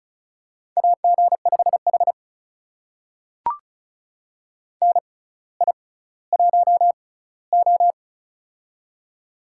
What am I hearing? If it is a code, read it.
Morse code, heard as a keyed tone, about 36 wpm, spaced as for 18 wpm, 690 Hz. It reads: AG5H NI1O
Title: USA Call Signs - 4 Characters - 6x Character Spacing - 35wpm